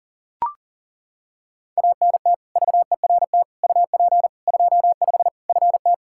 Morse code, heard as a keyed tone, about 40 wpm, 700 Hz. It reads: ANT VERT UP 25 FT
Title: Practice QSO - Exchange 2 - Info - 40wpm